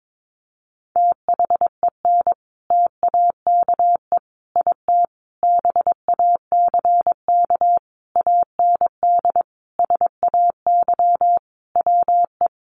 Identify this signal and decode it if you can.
Morse code, heard as a keyed tone, about 22 wpm, 700 Hz. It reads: THEN TAKE IT BACK AND SAY WE
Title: Sentences from Top 1500 Words 22wpm